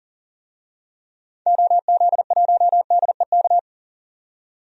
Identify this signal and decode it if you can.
Morse code, heard as a keyed tone, about 40 wpm, 700 Hz. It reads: OZ1DEK
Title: DXCC-100 Call Signs - 6 Characters - 40wpm